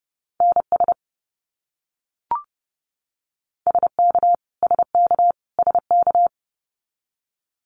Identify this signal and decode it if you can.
Morse code, heard as a keyed tone, about 30 wpm, 700 Hz. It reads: NS SK SK SK
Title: Canadian Provinces and Territories - Repeated 3x - 30wpm